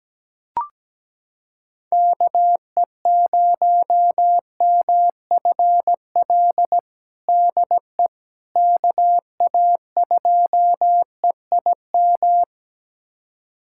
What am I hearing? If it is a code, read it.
Morse code, heard as a keyed tone, about 17 wpm, 700 Hz. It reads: KE0MFL DE KA2EIM
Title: USA Callsign DE Callsign - 6 Characters - 17wpm